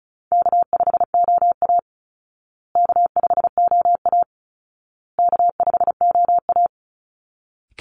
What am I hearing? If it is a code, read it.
Morse code, heard as a keyed tone, about 34 wpm, 700 Hz. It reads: K5OA K5OA K5OA
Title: Call Signs Repeated 2x - 35wpm